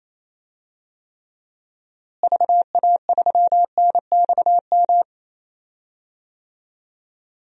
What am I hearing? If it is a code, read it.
Morse code, heard as a keyed tone, about 28 wpm, 700 Hz. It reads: VA3NXM
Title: DXCC-100 Call Signs - 6 Characters - Repeated 2x - 28wpm